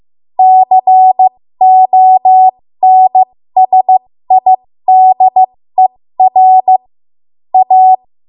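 Morse code, heard as a keyed tone, about 15 wpm, spaced as for 13 wpm, 750 Hz. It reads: CONSIDER A
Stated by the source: 13 wpm Morse code practice archive transmission.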